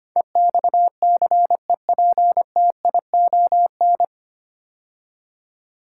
Morse code, heard as a keyed tone, about 25 wpm, 700 Hz. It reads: EXCEPTION
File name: practice / 25wpm / 9 Letter Words - First 3 Letters Given - 25wpm